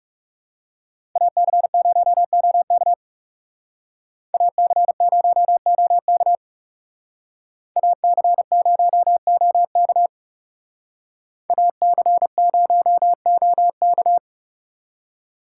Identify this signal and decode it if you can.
Morse code, heard as a keyed tone, about 37 wpm, 700 Hz. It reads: AC0OK AC0OK AC0OK AC0OK
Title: Speed Racing - USA Call Signs 30wpm